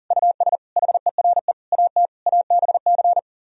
Morse code, heard as a keyed tone, about 51 wpm, 700 Hz. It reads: US HERE AT ABC